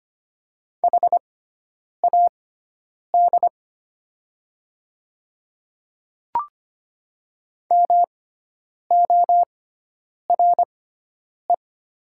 Morse code, heard as a keyed tone, about 25 wpm, spaced as for 9 wpm, 690 Hz. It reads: HAD MORE
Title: 6x Character Spacing - Top 100 Words - 25wpm